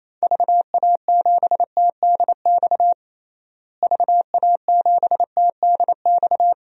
Morse code, heard as a keyed tone, about 28 wpm, 700 Hz. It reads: VA7TDX VA7TDX
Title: DXCC-100 Call Signs - 6 Characters - Repeated 2x - 28wpm